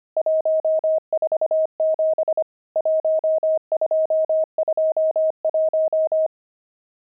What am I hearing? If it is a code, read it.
Morse code, heard as a keyed tone, about 25 wpm, 630 Hz. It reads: 147 1221
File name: pangram-fs-25-cs-25-nums14